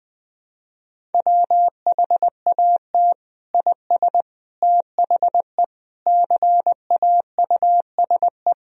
Morse code, heard as a keyed tone, about 21 wpm, 700 Hz. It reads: WHAT IS THE CAUSE
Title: Sentences from Top 1000 Words - Review Part 1 - 20wpm